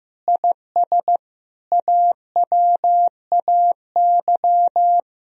Morse code, heard as a keyed tone, about 15 wpm, 700 Hz. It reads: IS AWAY
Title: Sentences from Top 1000 Words - Review Part 3 - 15wpm